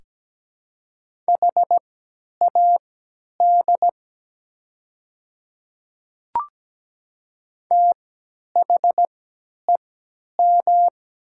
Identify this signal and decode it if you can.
Morse code, heard as a keyed tone, about 17 wpm, spaced as for 10 wpm, 700 Hz. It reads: HAD THEM
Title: 3x Character Spacing - Top 100 Words - 17wpm